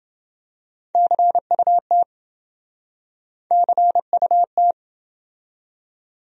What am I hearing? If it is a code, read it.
Morse code, heard as a keyed tone, about 29 wpm, 700 Hz. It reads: CUT CUT